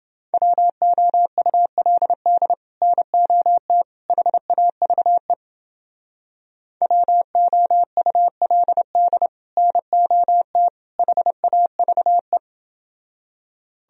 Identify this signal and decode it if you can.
Morse code, heard as a keyed tone, about 30 wpm, 700 Hz. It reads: WOULD NOT HAVE WOULD NOT HAVE